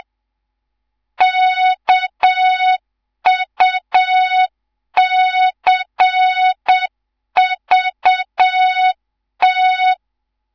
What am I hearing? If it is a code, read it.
Morse code, heard as a keyed tone, about 7 wpm, 3700 Hz. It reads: KUCVT
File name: Morze-HA2MN / hang / 11 GYAKORLAT